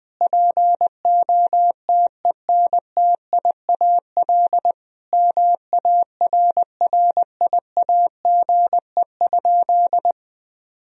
Morse code, heard as a keyed tone, about 20 wpm, 700 Hz. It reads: POTENTIAL MARRIAGE?
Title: Sentences from Top 1000 Words - Review Part 4 - 20wpm